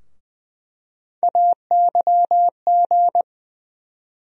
Morse code, heard as a keyed tone, about 20 wpm, 700 Hz. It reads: AYG
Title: USA Call Sign Postfixes - 3 Letters 20wpm